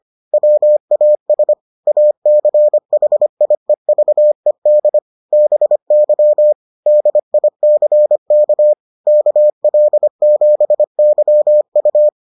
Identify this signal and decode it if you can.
Morse code, heard as a keyed tone, about 25 wpm, 600 Hz. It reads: WAS ACHIEVED BY DICK KL7YU